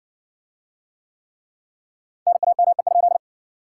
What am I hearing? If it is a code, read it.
Morse code, heard as a keyed tone, about 60 wpm, 700 Hz. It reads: NAME?